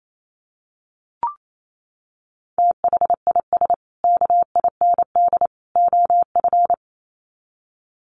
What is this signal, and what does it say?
Morse code, heard as a keyed tone, about 28 wpm, 700 Hz. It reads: THIS KIND OF
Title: Sets of 3 Words 28wpm